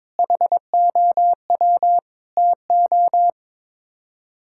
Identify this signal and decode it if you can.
Morse code, heard as a keyed tone, about 22 wpm, 700 Hz. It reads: HOW TO
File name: practice / 22wpm / Sets of 6 Words - 22wpm